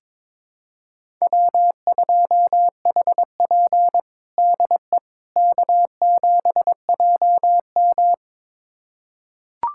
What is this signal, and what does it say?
Morse code, heard as a keyed tone, about 22 wpm, 700 Hz. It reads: W2HP DE K7JM
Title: USA Callsign DE Callsign - 4 Characters - 22wpm